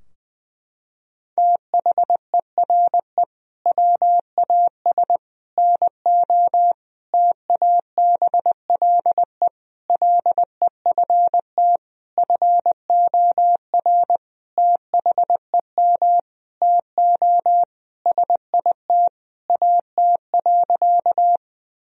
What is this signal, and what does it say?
Morse code, heard as a keyed tone, about 21 wpm, 700 Hz. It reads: THERE WAS NO TABLE LEFT FOR THEM TO SIT AT.
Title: Sentences from Top 1000 Words - Review Part 2 - 20wpm